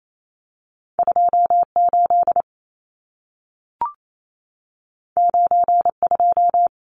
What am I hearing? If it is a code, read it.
Morse code, heard as a keyed tone, about 28 wpm, 700 Hz. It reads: 28 92
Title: Numbers - 2-Digit 28wpm